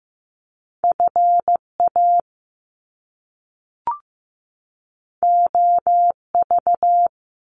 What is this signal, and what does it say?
Morse code, heard as a keyed tone, about 15 wpm, 700 Hz. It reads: FA OV
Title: Top Two Letter Combinations in Words 15wpm